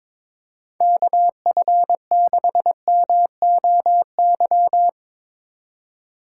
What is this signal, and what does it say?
Morse code, heard as a keyed tone, about 22 wpm, 700 Hz. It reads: KF6MOY
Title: USA Call Signs - 6 Characters - 4x Character Spacing - 22wpm